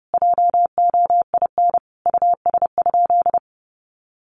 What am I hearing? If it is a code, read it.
Morse code, heard as a keyed tone, about 30 wpm, 700 Hz. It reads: JOIN US?
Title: Sentences from Top 1400 Words 30wpm